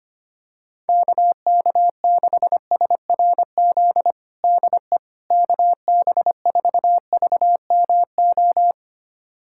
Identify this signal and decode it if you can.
Morse code, heard as a keyed tone, about 25 wpm, 700 Hz. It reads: KK6SRZ DE KB4VMO
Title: USA Callsign DE Callsign - 6 Characters - 25wpm